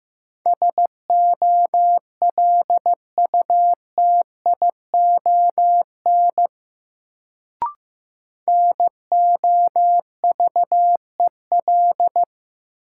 Morse code, heard as a keyed tone, about 15 wpm, 700 Hz. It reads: SOLUTION NOVEL
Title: Sentences from Top 1300 Words - New Word Drill - 15wpm